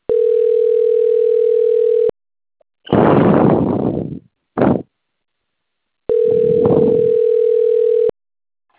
Ringback tone (North America), 440 + 480 Hz.
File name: out- - - - - .5367